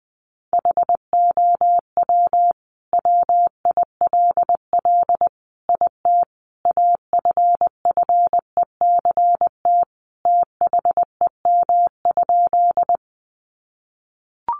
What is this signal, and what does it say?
Morse code, heard as a keyed tone, about 20 wpm, 700 Hz. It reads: HOW WILL IT AFFECT THEM?
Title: Sentences from Top 900 Words 20wpm